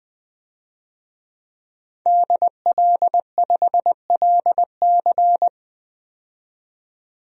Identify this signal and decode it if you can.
Morse code, heard as a keyed tone, about 20 wpm, 700 Hz. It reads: DL5LC
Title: DXCC-100 Call Signs - 5 Characters - 20wpm